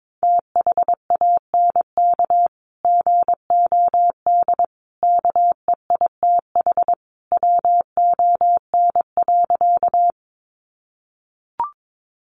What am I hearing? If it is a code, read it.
Morse code, heard as a keyed tone, about 22 wpm, 700 Hz. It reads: THANK GOD KEITH WON.
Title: A Sound Of Thunder 22wpm